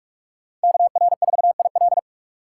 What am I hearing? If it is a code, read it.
Morse code, heard as a keyed tone, about 46 wpm, 700 Hz. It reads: KR4IL
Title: USA Call Signs - 5 Characters - 3x Character Spacing - 45wpm